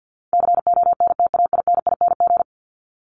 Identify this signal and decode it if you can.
Morse code, heard as a keyed tone, about 50 wpm, 700 Hz. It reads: CONTAINING
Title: ING Suffix - Top 100 Words with Base Words 50wpm